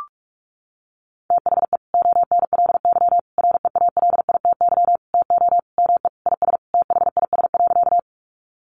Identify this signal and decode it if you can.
Morse code, heard as a keyed tone, about 44 wpm, 700 Hz. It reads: THE ONLY REALITY TO ME IS THIS.
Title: Sentences from Top 1200 Words 45wpm